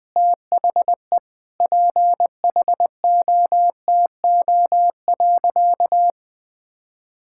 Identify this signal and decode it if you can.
Morse code, heard as a keyed tone, about 20 wpm, 700 Hz. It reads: THE PHOTO.
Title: Sentences with 4-Word Phrases - 20wpm